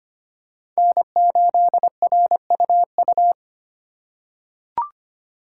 Morse code, heard as a keyed tone, about 25 wpm, 700 Hz. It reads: N8RUU
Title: USA Call Signs - 5 Characters - 4x Character Spacing - 25wpm